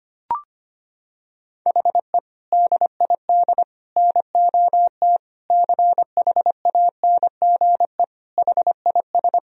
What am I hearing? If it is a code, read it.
Morse code, heard as a keyed tone, about 25 wpm, 700 Hz. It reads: HE DID NOT CHANGE HIS
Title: Sentences from Top 1000 Words - Review Part 1 - 25wpm